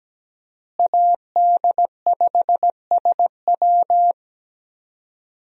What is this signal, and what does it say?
Morse code, heard as a keyed tone, about 17 wpm, 700 Hz. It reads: AD5SW
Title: USA Call Signs - 5 Characters - Repeated 3x - 17wpm